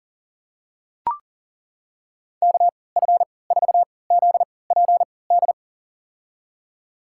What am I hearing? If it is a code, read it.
Morse code, heard as a keyed tone, about 40 wpm, spaced as for 23 wpm, 700 Hz. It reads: KF4ZPD
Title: USA Call Signs - 6 Characters - 3x Character Spacing - 40wpm